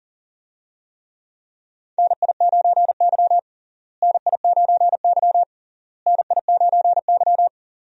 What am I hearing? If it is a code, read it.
Morse code, heard as a keyed tone, about 37 wpm, 700 Hz. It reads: NI9Y NI9Y NI9Y
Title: USA Call Signs - 4 Characters - Repeated 3x - 40wpm